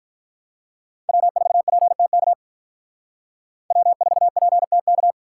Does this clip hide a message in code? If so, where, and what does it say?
Yes: Morse code from 1.1 to 5.1 s, heard as a keyed tone, about 45 wpm, 700 Hz. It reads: W4PTK W4PTK